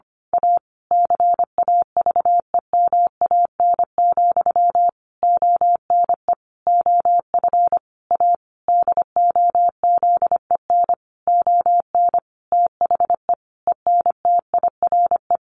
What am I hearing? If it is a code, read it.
Morse code, heard as a keyed tone, about 25 wpm, 700 Hz. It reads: A CAVEMAN, ONE OF A DOZEN ON THE ENTIRE
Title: A Sound Of Thunder 25wpm